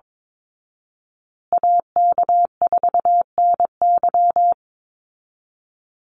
Morse code, heard as a keyed tone, about 22 wpm, 700 Hz. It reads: AK4NY
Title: Call Signs Repeated 2x - 22wpm